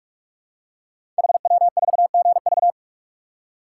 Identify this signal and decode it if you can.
Morse code, heard as a keyed tone, about 44 wpm, 700 Hz. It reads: HW4GV
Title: Code Groups - 5 Letter:Numbers - 45wpm